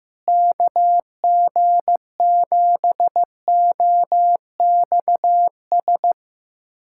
Morse code, heard as a keyed tone, about 15 wpm, 700 Hz. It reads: KG7OXS
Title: USA Callsign DE Callsign 15wpm